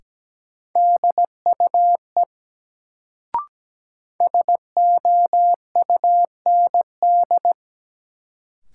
Morse code, heard as a keyed tone, about 17 wpm, 700 Hz. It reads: DUE SOUND
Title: Sentences from Top 900 Words - New Word Drill - 17wpm